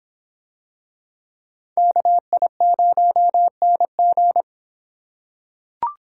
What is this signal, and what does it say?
Morse code, heard as a keyed tone, about 25 wpm, 700 Hz. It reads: KI0NG